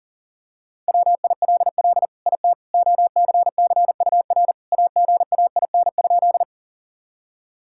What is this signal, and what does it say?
Morse code, heard as a keyed tone, about 40 wpm, 700 Hz. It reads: WILL IT OCCUR AGAIN?
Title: Sentences from Top 1000 Words - Review Part 6 - 40wpm